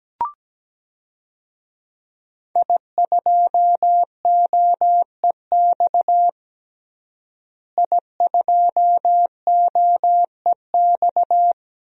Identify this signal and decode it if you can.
Morse code, heard as a keyed tone, about 17 wpm, 700 Hz. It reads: I2OEX I2OEX
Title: DXCC-100 Call Signs - 5 Characters - Repeated 2x - 17wpm